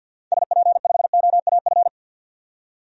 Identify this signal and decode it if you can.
Morse code, heard as a keyed tone, about 49 wpm, 700 Hz. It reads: SP5OAF
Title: DXCC-100 Call Signs - 6 Characters - 50wpm